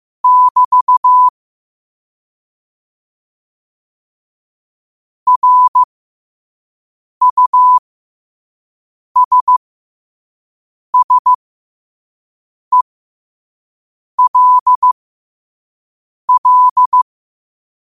Morse code, heard as a keyed tone, about 15 wpm, spaced as for 5 wpm, 1000 Hz. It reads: <BT> RUSSELL